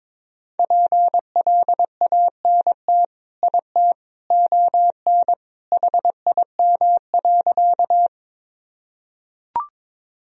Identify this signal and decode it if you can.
Morse code, heard as a keyed tone, about 22 wpm, 700 Hz. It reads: PLANT IT ON HIM.
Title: Sentences from Top 1000 Words - Review Part 1 - 22wpm